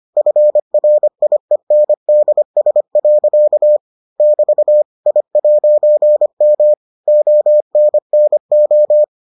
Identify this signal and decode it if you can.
Morse code, heard as a keyed tone, about 25 wpm, 600 Hz. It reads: FRIENDS. <BT> I'M ONNO